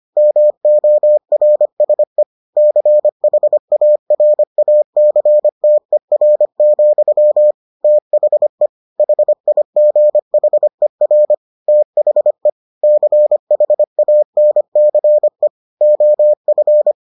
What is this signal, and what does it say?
Morse code, heard as a keyed tone, about 25 wpm, 600 Hz. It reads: MORSE CHARACTER, THE HIGHER THE CHANCE OF